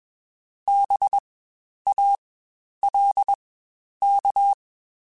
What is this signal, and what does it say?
Morse code, heard as a keyed tone, about 21 wpm, spaced as for 10 wpm, 800 Hz. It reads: BALK
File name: Spaced Alphas / Spaced Alphas mp3s / Bs sp @20